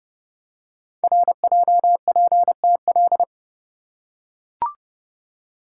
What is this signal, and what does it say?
Morse code, heard as a keyed tone, about 28 wpm, 700 Hz. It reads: RJPTL
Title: Code Groups - 5 Letters - 30wpm